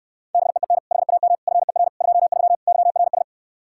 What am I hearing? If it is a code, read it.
Morse code, heard as a keyed tone, about 67 wpm, 700 Hz. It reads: BEEN HAM FER 23 YRS